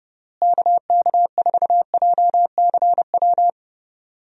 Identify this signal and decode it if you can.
Morse code, heard as a keyed tone, about 30 wpm, 700 Hz. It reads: KK4JCW
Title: USA Call Signs - 6 Characters - Repeated 2x - 30wpm